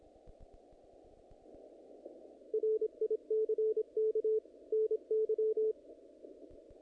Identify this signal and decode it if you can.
Morse code, heard as a keyed tone, about 26 wpm, 440 Hz, with radio noise. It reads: RICK NY
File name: contests / naqp-cw-k5tr-op-k5pi- -2000